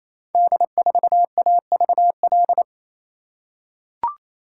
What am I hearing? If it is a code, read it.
Morse code, heard as a keyed tone, about 27 wpm, 700 Hz. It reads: D4AVL
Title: Code Groups - 5 Letter:Numbers - 28wpm